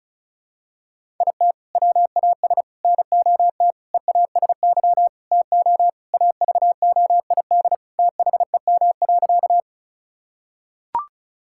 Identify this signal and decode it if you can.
Morse code, heard as a keyed tone, about 34 wpm, 700 Hz. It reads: IT WAS NOT EASY TO AVOID THEM.
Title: Sentences from Top 1000 Words - Review Part 6 - 35wpm